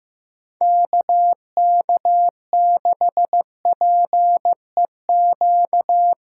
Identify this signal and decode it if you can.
Morse code, heard as a keyed tone, about 15 wpm, 700 Hz. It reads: KK6PEQ
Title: USA Callsign DE Callsign - 6 Characters - 15wpm